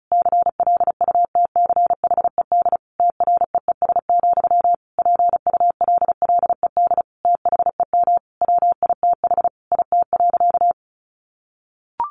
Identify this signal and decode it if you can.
Morse code, heard as a keyed tone, about 34 wpm, 700 Hz. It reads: CLUTCHED TREES, PULLED THEM WITH IT.
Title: A Sound Of Thunder 35wpm